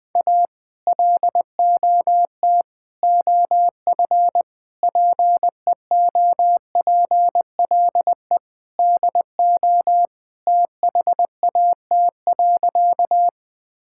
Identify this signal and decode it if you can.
Morse code, heard as a keyed tone, about 20 wpm, 700 Hz. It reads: A LOT OF PEOPLE DO THAT.
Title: Sentences with 5-Word Phrases - Spoken Beginning - 20wpm